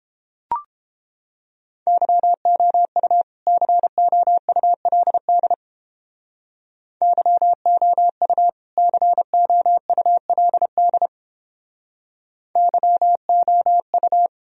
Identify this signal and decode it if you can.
Morse code, heard as a keyed tone, about 31 wpm, 700 Hz. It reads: YOU COULD YOU COULD YOU